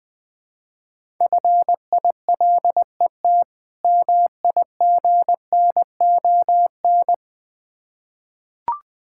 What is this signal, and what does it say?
Morse code, heard as a keyed tone, about 21 wpm, 690 Hz. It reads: FILET MIGNON
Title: Difficult Words with Hints - 20wpm